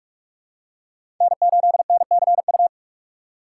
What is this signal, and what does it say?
Morse code, heard as a keyed tone, about 46 wpm, 700 Hz. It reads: N8NCU
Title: USA Call Signs - 5 Characters - 4x Character Spacing - 45wpm